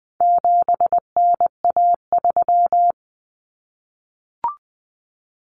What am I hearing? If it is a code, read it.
Morse code, heard as a keyed tone, about 20 wpm, 700 Hz. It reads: 7NA3
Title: Common Cut Numbers - 4-Digit 20wpm